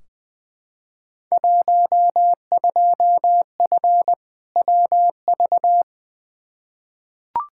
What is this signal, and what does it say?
Morse code, heard as a keyed tone, about 20 wpm, 700 Hz. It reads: 12F WV
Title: Field Day Exchanges Encore - 20wpm